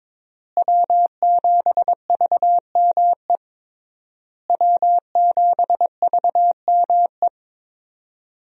Morse code, heard as a keyed tone, about 22 wpm, 700 Hz. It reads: W7VME W7VME
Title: USA Call Signs - 5 Characters - Repeated 3x - 22wpm